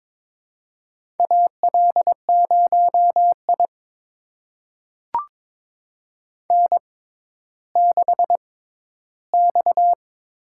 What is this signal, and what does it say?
Morse code, heard as a keyed tone, about 22 wpm, spaced as for 13 wpm, 700 Hz. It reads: AL0I N6X
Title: USA Call Signs - 4 Characters - 6x Character Spacing - 22wpm